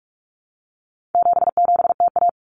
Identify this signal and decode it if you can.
Morse code, heard as a keyed tone, about 44 wpm, 700 Hz. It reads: 77TA
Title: Common Cut Numbers - 4-Digit 45wpm